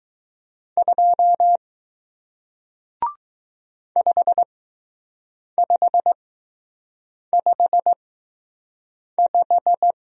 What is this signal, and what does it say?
Morse code, heard as a keyed tone, about 20 wpm, spaced as for 6 wpm, 700 Hz. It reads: 2 5555